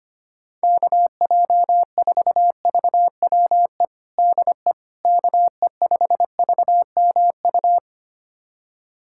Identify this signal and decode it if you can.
Morse code, heard as a keyed tone, about 25 wpm, 700 Hz. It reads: KJ4VWE DE KE5VMU
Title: USA Callsign DE Callsign - 6 Characters - 25wpm